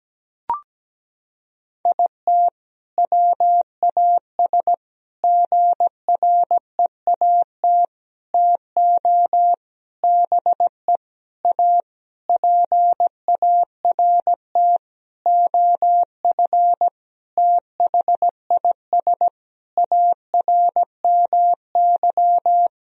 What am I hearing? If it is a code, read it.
Morse code, heard as a keyed tone, about 17 wpm, 700 Hz. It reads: IT WAS GREAT TO BE A PART OF THIS ARMY
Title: Sentences from Top 1000 Words - Review Part 2 - 17wpm